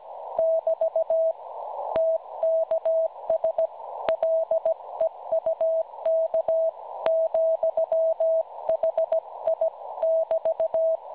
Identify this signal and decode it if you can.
Morse code, heard as a keyed tone, about 17 wpm, 650 Hz. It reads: <BT> TKS LEUK, HI <BT>
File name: cwronde / websdr recording .6kHz